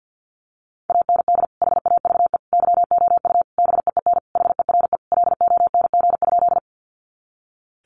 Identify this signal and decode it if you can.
Morse code, heard as a keyed tone, about 50 wpm, 700 Hz. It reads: AND HAVE YOU BEEN HERE LONG?
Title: Sentences from Top 200 Words 50wpm